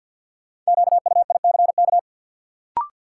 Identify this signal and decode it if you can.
Morse code, heard as a keyed tone, about 52 wpm, 700 Hz. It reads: QUICK
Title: Sentences from Top 500 Words - New Word Drill - 50wpm